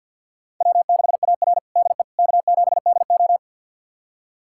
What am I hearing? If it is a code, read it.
Morse code, heard as a keyed tone, about 51 wpm, 700 Hz. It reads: W6AR DE K7DO
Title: USA Callsign DE Callsign - 4 Characters - 50wpm